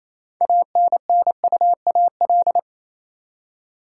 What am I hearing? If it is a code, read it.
Morse code, heard as a keyed tone, about 28 wpm, 700 Hz. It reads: ANNUAL
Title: Six Letter Words - 28wpm